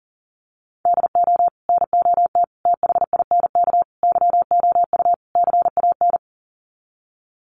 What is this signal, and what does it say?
Morse code, heard as a keyed tone, about 40 wpm, 700 Hz. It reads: DO NOT THINK YOU CAN
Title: Sets of 5 Words 40wpm